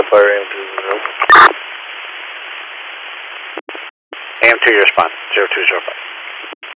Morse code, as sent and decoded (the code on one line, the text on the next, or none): none